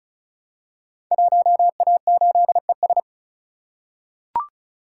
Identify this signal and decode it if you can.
Morse code, heard as a keyed tone, about 36 wpm, 700 Hz. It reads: 1A8ES
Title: Code Groups - 5 Letter:Numbers - 35wpm